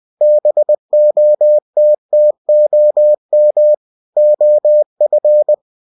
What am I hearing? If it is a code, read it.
Morse code, heard as a keyed tone, about 20 wpm, 600 Hz. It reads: BOTTOM OF
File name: cw-books / winnie-the-pooh / winnie-20wpm-page-0038